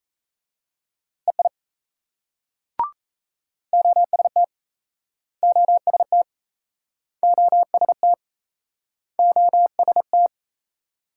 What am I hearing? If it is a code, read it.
Morse code, heard as a keyed tone, about 35 wpm, 700 Hz. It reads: EI OST OST OST OST